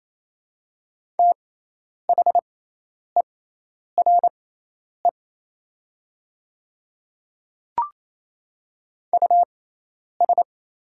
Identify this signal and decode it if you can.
Morse code, heard as a keyed tone, about 28 wpm, spaced as for 10 wpm, 700 Hz. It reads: THERE US
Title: 6x Character Spacing - Top 200 Words - 28wpm